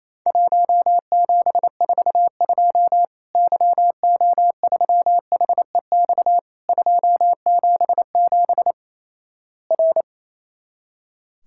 Morse code, heard as a keyed tone, about 28 wpm, 700 Hz. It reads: 1742 YO3HEX 277 R
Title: Worked All Europe DX Contest - QTC Exchange - 28wpm